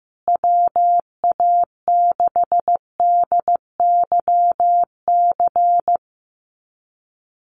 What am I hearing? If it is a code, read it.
Morse code, heard as a keyed tone, about 15 wpm, 700 Hz. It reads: WA6DYC
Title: USA Call Signs Repeated 3x 15wpm